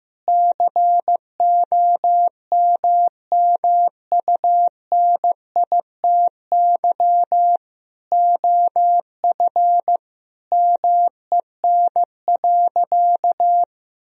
Morse code, heard as a keyed tone, about 15 wpm, 700 Hz. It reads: COMMUNITY OF MEN.
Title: Sentences from Top 1000 Words - Review Part 4 - 15wpm